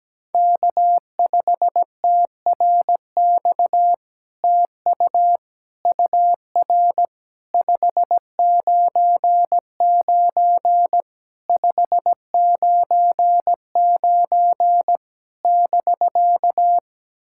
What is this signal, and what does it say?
Morse code, heard as a keyed tone, about 17 wpm, 700 Hz. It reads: K5TRX TU UR 599 599 <BK>
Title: SOTA QSO Practice 17wpm